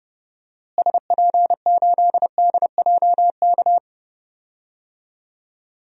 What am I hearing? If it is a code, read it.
Morse code, heard as a keyed tone, about 30 wpm, 700 Hz. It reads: SP8DJK
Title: DXCC-100 Call Signs - 6 Characters - Repeated 3x - 30wpm